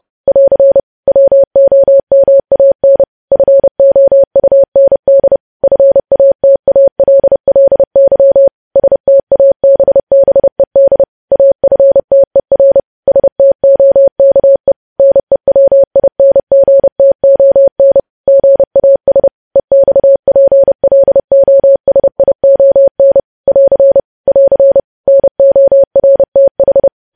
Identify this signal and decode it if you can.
Morse code, heard as a keyed tone, about 30 wpm, 550 Hz. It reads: <AR> WOMAN FOUND FATALLY STABBED AFTER STOKE NEWINGTON GAS EXPLOSION <AR> <AR> NORTH
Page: news headlines in morse code at 30 wpm